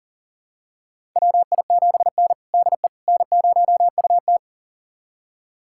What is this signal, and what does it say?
Morse code, heard as a keyed tone, about 40 wpm, 690 Hz. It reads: WI7N DE N0UT
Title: USA Callsign DE Callsign - 4 Characters - 40wpm